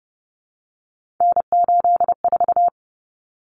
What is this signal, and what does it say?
Morse code, heard as a keyed tone, about 30 wpm, 700 Hz. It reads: N84
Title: Common Cut Numbers - 3-Digit 30wpm